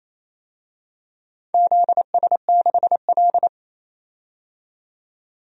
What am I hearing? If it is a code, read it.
Morse code, heard as a keyed tone, about 28 wpm, 700 Hz. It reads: ZS6L
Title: DXCC-100 Call Signs - 4 Characters - 28wpm